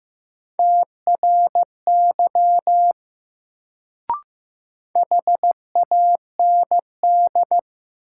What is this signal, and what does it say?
Morse code, heard as a keyed tone, about 15 wpm, 700 Hz. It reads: TRY HAND
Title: Sentences from Top 200 Words - New Word Drill - 15wpm